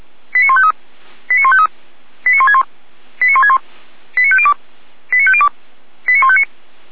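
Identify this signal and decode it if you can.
Five-tone selective call (ZVEI), heard as quick runs of five tones, tones of about 70 ms. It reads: 89153 89153 89161 89161 89502 89502 89159
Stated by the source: ZVEI five-tone call 89160 received by SDR, from the Feuerwehr Orlamünde fire brigade.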